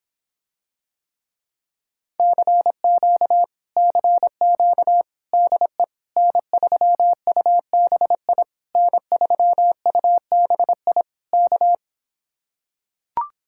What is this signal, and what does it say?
Morse code, heard as a keyed tone, about 26 wpm, 700 Hz. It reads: CQ CQ DE N3UBI N3UBI K